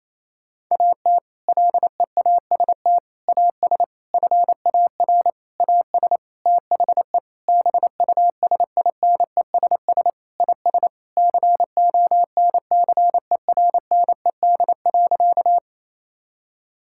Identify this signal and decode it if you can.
Morse code, heard as a keyed tone, about 28 wpm, 700 Hz. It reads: AT LEAST AS FAR AS THE BUSINESS IS CONCERNED.
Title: Sentences from Top 1000 Words - Review Part 6 - 28wpm